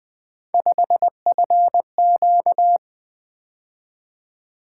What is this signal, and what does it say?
Morse code, heard as a keyed tone, about 20 wpm, 700 Hz. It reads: 5FQ
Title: Three Letter-Number Combinations - 20wpm